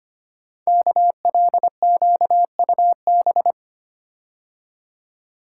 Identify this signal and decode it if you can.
Morse code, heard as a keyed tone, about 25 wpm, 700 Hz. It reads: KLQUB
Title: Code Groups - 5 Letter:Numbers - 25wpm